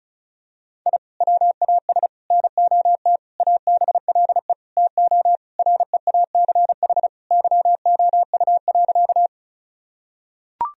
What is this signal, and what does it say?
Morse code, heard as a keyed tone, about 36 wpm, 700 Hz. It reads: I WAS NOT ABLE TO REACH YOU.
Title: Sentences from Top 1000 Words - Review Part 2 - 35wpm